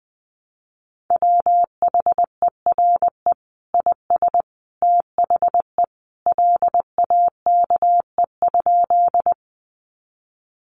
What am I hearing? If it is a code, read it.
Morse code, heard as a keyed tone, about 20 wpm, 700 Hz. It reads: WHERE IS THE LAKE?
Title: Sentences from Top 1100 Words 20wpm